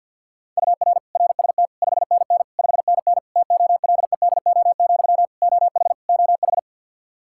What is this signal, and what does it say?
Morse code, heard as a keyed tone, about 52 wpm, 700 Hz. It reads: UR RST 5NN 5NN TOLEDO, OH OH
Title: Practice QSO - Exchange 1 - Beginning - 50wpm